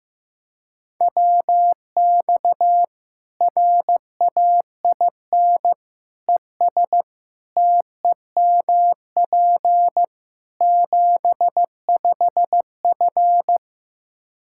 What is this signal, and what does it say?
Morse code, heard as a keyed tone, about 15 wpm, 700 Hz. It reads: WX RAIN ES TEMP 75F
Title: Practice QSO - Exchange 2 - Info - 15wpm